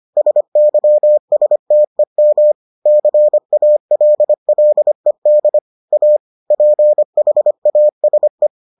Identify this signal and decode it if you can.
Morse code, heard as a keyed tone, about 25 wpm, 600 Hz. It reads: SYSTEM CALLED A PHASE